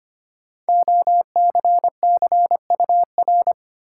Morse code, heard as a keyed tone, about 25 wpm, 700 Hz. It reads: OCCUR
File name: 5 Letter Words - First 3 Letters Given - 25wpm